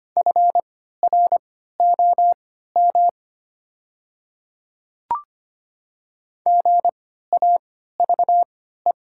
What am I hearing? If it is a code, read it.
Morse code, heard as a keyed tone, about 25 wpm, spaced as for 14 wpm, 700 Hz. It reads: FROM GAVE
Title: 3x Character Spacing - Top 500 Words - 25wpm